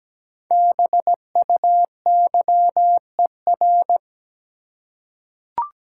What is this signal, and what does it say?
Morse code, heard as a keyed tone, about 17 wpm, 700 Hz. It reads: BUYER
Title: Five Letter Words - 17wpm